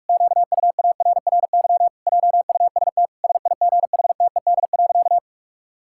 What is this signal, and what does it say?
Morse code, heard as a keyed tone, about 46 wpm, 700 Hz. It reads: QUARRY JUST SIGHTED.
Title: Pangram Challenge v2 - 45wpm